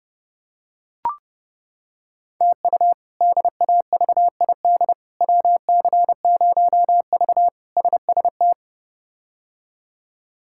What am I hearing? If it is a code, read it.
Morse code, heard as a keyed tone, about 30 wpm, 700 Hz. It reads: TU DAVID WC0V SST
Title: K1USN SST QSO Practice - 30wpm